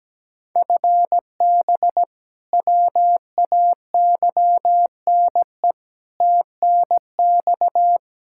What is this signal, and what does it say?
Morse code, heard as a keyed tone, about 17 wpm, 700 Hz. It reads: FB WAYNE TNX
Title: Practice QSO - Exchange 3 - More Info - 17wpm